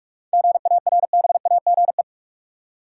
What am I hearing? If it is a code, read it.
Morse code, heard as a keyed tone, about 46 wpm, 700 Hz. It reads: GARBAGE
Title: Seven Letter Words - 45wpm